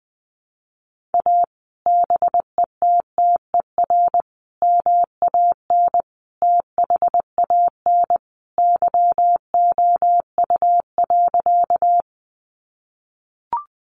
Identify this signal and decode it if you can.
Morse code, heard as a keyed tone, about 20 wpm, 700 Hz. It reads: A BETTER MAN THAN YOU.
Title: Sentences from Top 400 Words 20wpm